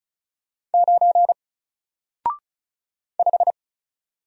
Morse code, heard as a keyed tone, about 36 wpm, 700 Hz. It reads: 9 5
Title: Single Letter-Number Alt 35wpm